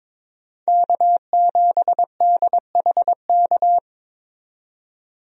Morse code, heard as a keyed tone, about 22 wpm, 700 Hz. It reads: K7DHK
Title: USA Call Signs - 5 Characters - 22wpm